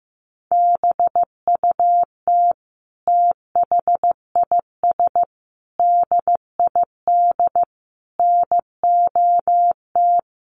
Morse code, heard as a keyed tone, about 15 wpm, 700 Hz. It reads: BUT THIS DID NOT
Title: Sentences from Top 1200 Words 15wpm